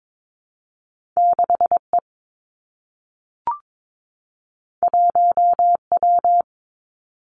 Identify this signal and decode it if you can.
Morse code, heard as a keyed tone, about 22 wpm, 700 Hz. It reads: 6 E 1 W
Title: Two Challenging Letter-Number Combinations 22wpm